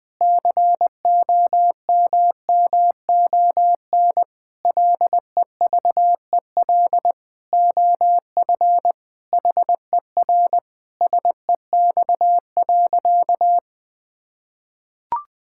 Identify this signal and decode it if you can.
Morse code, heard as a keyed tone, about 20 wpm, 700 Hz. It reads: COMMON LEVEL OF HER SEX.
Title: Sentences from Top 1600 Words 20wpm